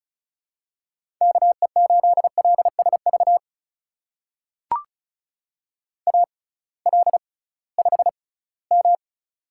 Morse code, heard as a keyed tone, about 36 wpm, spaced as for 20 wpm, 700 Hz. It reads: KE8LSV AL5M
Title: USA Call Signs - 6x Character Spacing - 35wpm